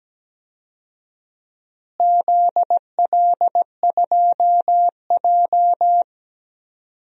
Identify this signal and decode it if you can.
Morse code, heard as a keyed tone, about 17 wpm, 700 Hz. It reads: ZL2J
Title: DXCC-100 Call Signs - 4 Characters - Repeated 3x - 17wpm